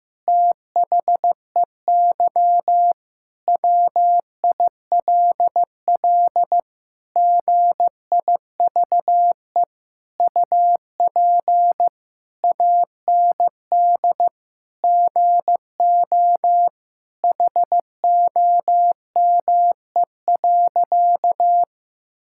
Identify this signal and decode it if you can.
Morse code, heard as a keyed tone, about 15 wpm, 700 Hz. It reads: THEY WILL GIVE UP AND GO HOME.
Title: Sentences from Top 1000 Words - Review Part 6 - 15wpm